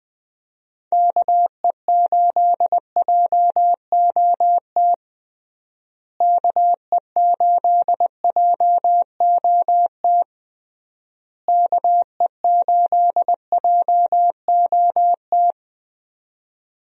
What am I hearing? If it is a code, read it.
Morse code, heard as a keyed tone, about 20 wpm, 700 Hz. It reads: KE8JOT KE8JOT KE8JOT
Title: USA Call Signs - 6 Characters - Repeated 3x - 20wpm